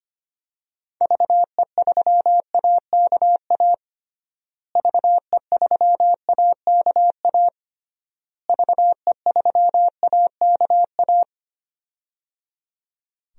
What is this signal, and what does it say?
Morse code, heard as a keyed tone, about 25 wpm, 700 Hz. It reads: VE3AKA VE3AKA VE3AKA
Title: DXCC-100 Call Signs - 6 Characters - Repeated 3x - 25wpm